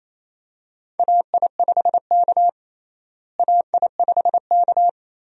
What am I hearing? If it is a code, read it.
Morse code, heard as a keyed tone, about 27 wpm, 700 Hz. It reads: AI5K AI5K
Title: USA Call Signs - 4 Characters - Repeated 2x - 28wpm